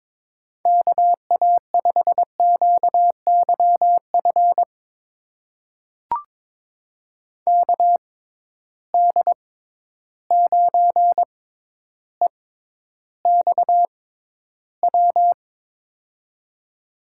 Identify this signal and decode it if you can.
Morse code, heard as a keyed tone, about 21 wpm, spaced as for 11 wpm, 700 Hz. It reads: KA5QYF KD9EXW
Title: USA Call Signs - 6 Characters - 6x Character Spacing - 22wpm